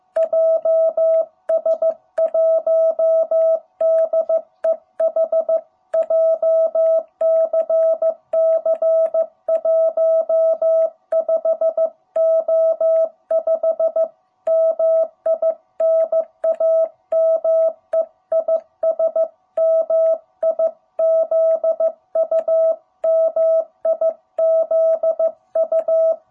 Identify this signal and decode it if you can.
Morse code, heard as a keyed tone, about 15 wpm, 660 Hz. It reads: JS1DEH JCC 15O5 MI NAME IS MIZU MIZU
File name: jh8cbh / 2021cw